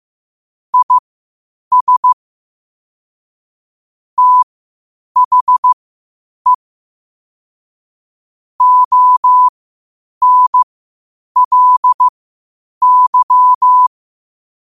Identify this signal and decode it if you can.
Morse code, heard as a keyed tone, about 15 wpm, spaced as for 8 wpm, 1000 Hz. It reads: IS THE ONLY
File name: qotdfiles / 8wpm